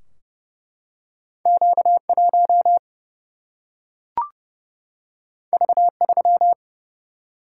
Morse code, heard as a keyed tone, about 28 wpm, 700 Hz. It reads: Q 1 V 3
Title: Single-Letter Single-Number 30wpm